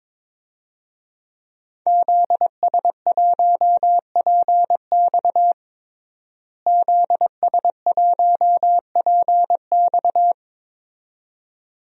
Morse code, heard as a keyed tone, about 22 wpm, 700 Hz. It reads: ZS1PX ZS1PX
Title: DXCC-100 Call Signs - 5 Characters - Repeated 2x - 22wpm